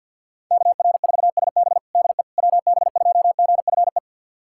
Morse code, heard as a keyed tone, about 52 wpm, 700 Hz. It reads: KR4SB DE WB2GFE
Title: USA Callsign DE Callsign 50wpm